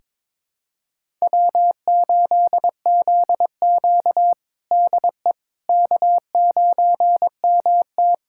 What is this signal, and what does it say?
Morse code, heard as a keyed tone, about 21 wpm, 700 Hz. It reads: W8ZQ DE K9MT
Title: USA Callsign DE Callsign - 4 Characters - 22wpm